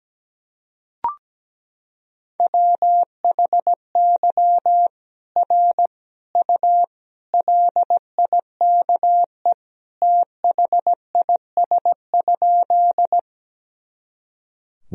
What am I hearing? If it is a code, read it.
Morse code, heard as a keyed tone, about 17 wpm, 700 Hz. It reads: WHY R U LIKE THIS?
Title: Sentences from Top 200 Words and Abbreviations 17wpm